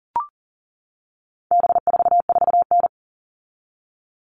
Morse code, heard as a keyed tone, about 40 wpm, 700 Hz. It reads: B44N
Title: All Cut Numbers - 4-Digit 40wpm